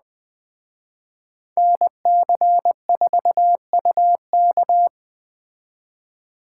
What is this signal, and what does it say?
Morse code, heard as a keyed tone, about 20 wpm, 700 Hz. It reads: NC4UK
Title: Speed Racing - USA 2x2 Call Signs 17wpm